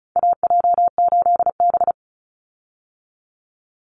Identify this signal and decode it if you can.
Morse code, heard as a keyed tone, about 34 wpm, 700 Hz. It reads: AJ8B
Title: Call Signs Repeated 3x - 35wpm